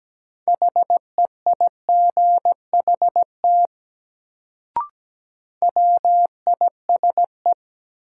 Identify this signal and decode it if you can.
Morse code, heard as a keyed tone, about 17 wpm, 700 Hz. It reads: HEIGHT WISE
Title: Sentences from Top 1600 Words - New Word Drill Alt - 17wpm